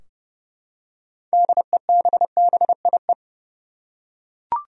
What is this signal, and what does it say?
Morse code, heard as a keyed tone, about 32 wpm, 700 Hz. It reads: DEBBIE
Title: USA Common Names - 30wpm